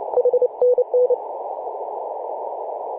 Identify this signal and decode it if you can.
Morse code, heard as a keyed tone, about 28 wpm, 500 Hz, with noise. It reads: HNN
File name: fm-f9ie-21-028mhz-cw